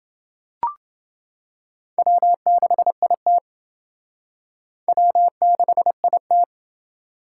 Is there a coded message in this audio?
Yes: Morse code from 2.0 to 6.4 s, heard as a keyed tone, about 28 wpm, 700 Hz. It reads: W6IT W6IT